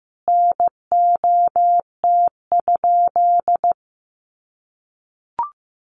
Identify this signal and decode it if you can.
Morse code, heard as a keyed tone, about 15 wpm, 700 Hz. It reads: NOT?
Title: Sentences from Top 1100 Words 15wpm